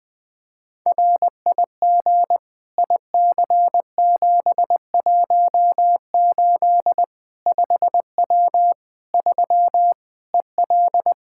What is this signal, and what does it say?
Morse code, heard as a keyed tone, about 20 wpm, 700 Hz. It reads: RIG IC718 5W 3 EL
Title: K1USN SST QSO Practice 20wpm